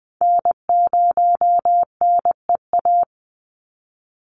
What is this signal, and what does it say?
Morse code, heard as a keyed tone, about 20 wpm, 700 Hz. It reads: N0NEA
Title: USA Call Signs 20wpm